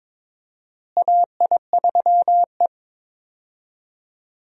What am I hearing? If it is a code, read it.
Morse code, heard as a keyed tone, about 21 wpm, 700 Hz. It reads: AI3E
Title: USA Call Signs - 4 Characters - Repeated 3x - 22wpm